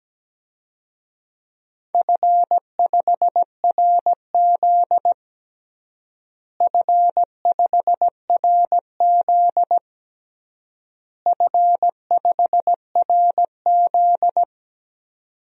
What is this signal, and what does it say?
Morse code, heard as a keyed tone, about 17 wpm, 700 Hz. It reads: F5RZ F5RZ F5RZ
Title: DXCC-100 Call Signs - 4 Characters - Repeated 3x - 17wpm